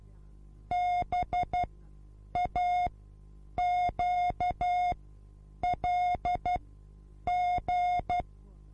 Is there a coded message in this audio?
Yes: Morse code from 0.7 to 8.2 s, heard as a keyed tone, about 12 wpm, spaced as for 8 wpm, 720 Hz. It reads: BAQLG